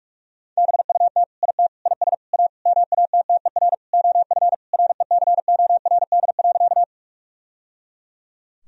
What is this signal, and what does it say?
Morse code, heard as a keyed tone, about 41 wpm, 700 Hz. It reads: BUT IT IS A MATTER OF RECORD.
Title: Sentences from Top 1000 Words - Review Part 3 - 45wpm